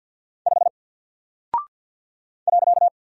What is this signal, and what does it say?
Morse code, heard as a keyed tone, about 49 wpm, 700 Hz. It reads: 5 .
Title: Single Character (Letter:Number:Punctuation:Prosign) Alt 50wpm